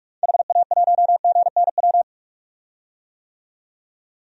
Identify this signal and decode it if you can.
Morse code, heard as a keyed tone, about 44 wpm, 690 Hz. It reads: HA1GNW
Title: DXCC-100 Call Signs - 6 Characters - 45wpm